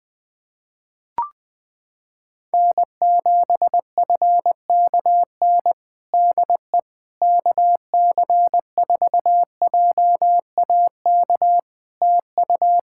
Morse code, heard as a keyed tone, about 20 wpm, 700 Hz. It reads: N7FKN DE KC4JAK TU
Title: Practice QSO - Exchange 4 - Ending - 20wpm